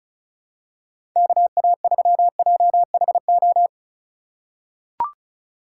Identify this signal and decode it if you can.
Morse code, heard as a keyed tone, about 36 wpm, 700 Hz. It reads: KA3JHO
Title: USA Call Signs - 3x Character Spacing - 35wpm